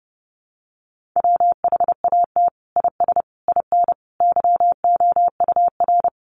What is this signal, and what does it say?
Morse code, heard as a keyed tone, about 30 wpm, 700 Hz. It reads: WHAT IS IN YOUR
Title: Sentences from Top 200 Words 30wpm